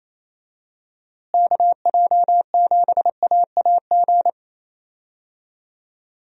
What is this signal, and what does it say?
Morse code, heard as a keyed tone, about 28 wpm, 700 Hz. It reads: KJ7AAG
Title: USA Call Signs - 6 Characters - Repeated 2x - 28wpm